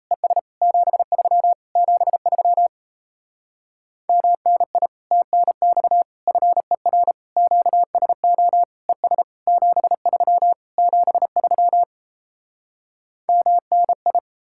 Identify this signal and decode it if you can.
Morse code, heard as a keyed tone, about 37 wpm, 700 Hz. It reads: ES 73 73 MNI TNX FER QSO ES 73 73 MNI